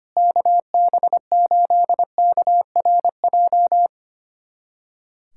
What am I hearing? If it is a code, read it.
Morse code, heard as a keyed tone, about 25 wpm, 700 Hz. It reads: KB8KRJ
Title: USA Callsign DE Callsign 25wpm